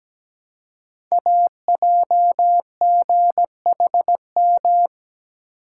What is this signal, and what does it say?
Morse code, heard as a keyed tone, about 17 wpm, 700 Hz. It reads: AJGHM
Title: Code Groups - 5 Letters - 17wpm